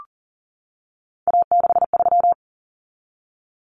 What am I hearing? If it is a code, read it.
Morse code, heard as a keyed tone, about 40 wpm, 700 Hz. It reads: A63
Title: Common Cut Numbers - 3-Digit 40wpm